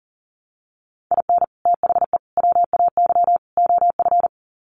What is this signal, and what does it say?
Morse code, heard as a keyed tone, about 40 wpm, 700 Hz. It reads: IN THE WAY OF
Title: Sets of 4 Words 40wpm